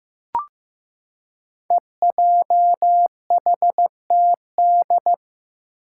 Morse code, heard as a keyed tone, about 15 wpm, 700 Hz. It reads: EJHTD
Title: Five Challenging Letter-Number Combinations 15wpm